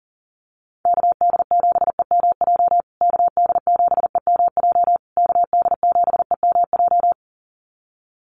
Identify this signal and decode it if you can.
Morse code, heard as a keyed tone, about 40 wpm, 700 Hz. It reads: KD7EMJ KD7EMJ KD7EMJ
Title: USA Call Signs Repeated 3x 40wpm